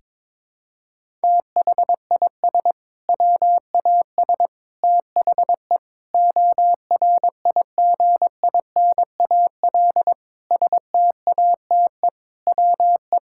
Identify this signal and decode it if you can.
Morse code, heard as a keyed tone, about 22 wpm, 700 Hz. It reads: THIS WAS THE ORIGINAL STATE WE
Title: Sentences from Top 1000 Words - Review Part 6 - 22wpm